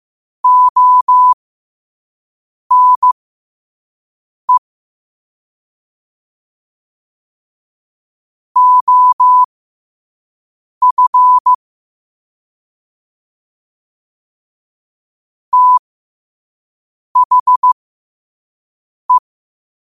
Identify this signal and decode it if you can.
Morse code, heard as a keyed tone, about 15 wpm, spaced as for 5 wpm, 1000 Hz. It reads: ONE OF THE
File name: qotdfiles / 5wpm